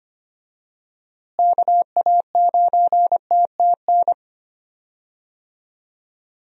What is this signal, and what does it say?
Morse code, heard as a keyed tone, about 25 wpm, 700 Hz. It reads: KA9TTN
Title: USA Call Signs - 6 Characters - Repeated 2x - 25wpm